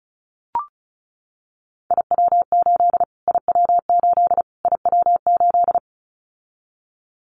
Morse code, heard as a keyed tone, about 34 wpm, 700 Hz. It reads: IW8 IW8 IW8
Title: Callsign Prefixes for Top 25 DXCC - Repeated 3x - 35wpm